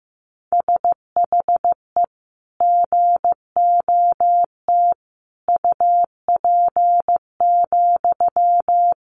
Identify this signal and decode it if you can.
Morse code, heard as a keyed tone, about 15 wpm, 700 Hz. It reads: SHE GOT UP,
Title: Sentences from Top 500 Words 15wpm